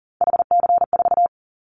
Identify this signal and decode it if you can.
Morse code, heard as a keyed tone, about 38 wpm, 700 Hz. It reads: HC4
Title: Callsign Prefixes for Top 100 DXCC - 40wpm